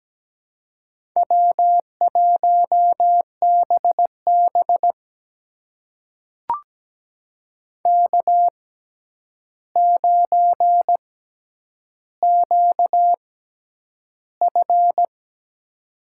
Morse code, heard as a keyed tone, about 17 wpm, spaced as for 9 wpm, 700 Hz. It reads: W1BB K9QF
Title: USA Call Signs - 4 Characters - 6x Character Spacing - 17wpm